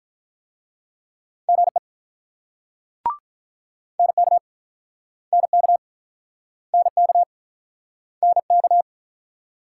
Morse code, heard as a keyed tone, about 48 wpm, 700 Hz. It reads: GE NK NK NK NK